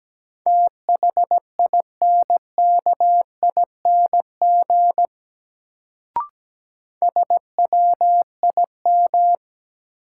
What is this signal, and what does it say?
Morse code, heard as a keyed tone, about 17 wpm, 700 Hz. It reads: THINKING SWIM
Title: Sentences from Top 1000 Words - Review Part 6 - New Word Drill - 17wpm